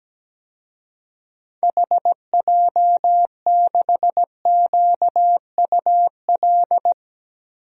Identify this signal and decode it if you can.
Morse code, heard as a keyed tone, about 17 wpm, 700 Hz. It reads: HJ6QUL
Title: DXCC-100 Call Signs - 6 Characters - Repeated 3x - 17wpm